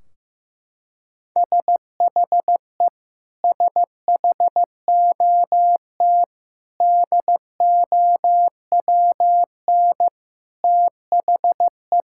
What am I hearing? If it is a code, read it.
Morse code, heard as a keyed tone, about 15 wpm, 700 Hz. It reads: SHE SHOT DOWN THE
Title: Sentences from Top 1600 Words 15wpm